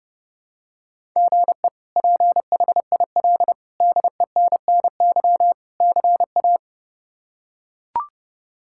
Morse code, heard as a keyed tone, about 30 wpm, 700 Hz. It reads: GE PHIL DENNY CA
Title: K1USN SST QSO Practice - 30wpm